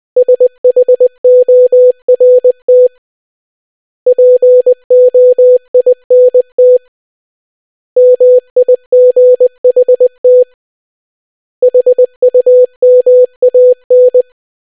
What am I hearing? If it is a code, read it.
Morse code, heard as a keyed tone, about 20 wpm, 500 Hz. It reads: SHORT POINT MIGHT HUMAN